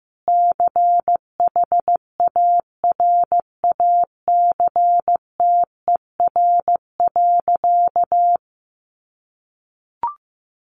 Morse code, heard as a keyed tone, about 15 wpm, 700 Hz. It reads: CHARACTER.
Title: Sentences from Top 1200 Words 15wpm